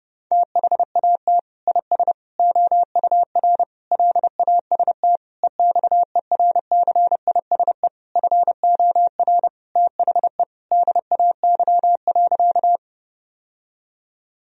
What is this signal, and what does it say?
Morse code, heard as a keyed tone, about 28 wpm, 700 Hz. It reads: THAT IS OUR LAST EXERCISE FOR THE DAY.
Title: Sentences from Top 1000 Words - Review Part 4 - 30wpm